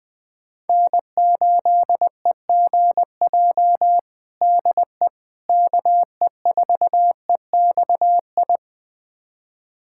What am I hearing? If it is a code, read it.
Morse code, heard as a keyed tone, about 20 wpm, 700 Hz. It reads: N8EGJ DE KE4EXI
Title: USA Callsign DE Callsign 20wpm